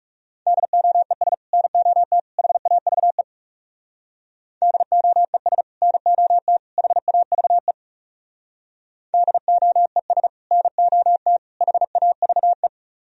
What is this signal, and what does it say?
Morse code, heard as a keyed tone, about 40 wpm, 700 Hz. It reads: DOES NOT HAVE DOES NOT HAVE DOES NOT HAVE